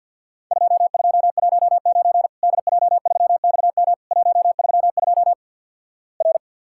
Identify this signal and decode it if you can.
Morse code, heard as a keyed tone, about 50 wpm, 700 Hz. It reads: 2219 DJ3XM 132 R
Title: Worked All Europe DX Contest - QTC Exchange - 50wpm